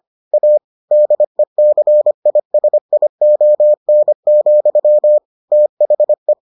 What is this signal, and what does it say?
Morse code, heard as a keyed tone, about 25 wpm, 600 Hz. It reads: A DECISION, THE